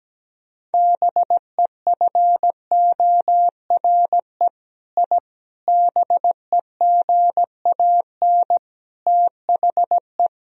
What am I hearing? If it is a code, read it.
Morse code, heard as a keyed tone, about 17 wpm, 700 Hz. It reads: BEFORE I BEGAN THE
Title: Sentences from Top 1000 Words - Review Part 4 - 17wpm